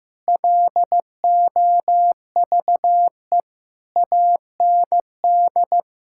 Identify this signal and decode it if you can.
Morse code, heard as a keyed tone, about 15 wpm, 700 Hz. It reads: LOVE AND
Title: Sentences from Top 1600 Words 15wpm